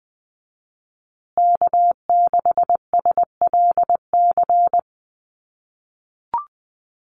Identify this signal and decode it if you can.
Morse code, heard as a keyed tone, about 20 wpm, 700 Hz. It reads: K6SLC
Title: USA Call Signs Repeated 2x 20wpm